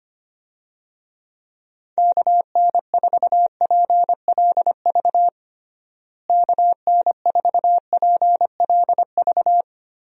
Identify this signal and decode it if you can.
Morse code, heard as a keyed tone, about 25 wpm, 700 Hz. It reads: KN4PLV KN4PLV
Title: USA Call Signs - 6 Characters - Repeated 3x - 25wpm